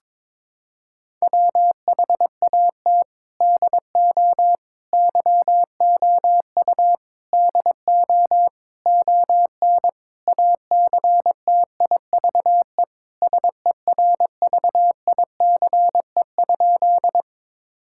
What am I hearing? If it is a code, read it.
Morse code, heard as a keyed tone, about 22 wpm, 700 Hz. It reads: WHAT DO YOU DO ON ACTIVE SERVICE?
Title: Sentences from Top 1000 Words - Review Part 6 - 22wpm